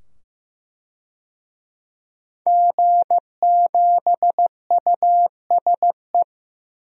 Morse code, heard as a keyed tone, about 15 wpm, 700 Hz. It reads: G7USE
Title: DXCC-100 Call Signs - 5 Characters - 15wpm